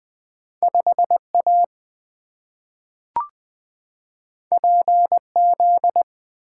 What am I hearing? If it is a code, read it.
Morse code, heard as a keyed tone, about 20 wpm, 700 Hz. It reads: 5 A P Z
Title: Two Letter-Number Combinations - 20wpm